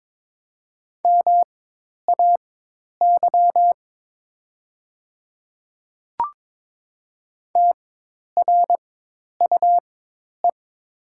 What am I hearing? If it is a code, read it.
Morse code, heard as a keyed tone, about 22 wpm, spaced as for 10 wpm, 700 Hz. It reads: MAY TRUE
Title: 4x Character Spacing - Top 500 Words - 22wpm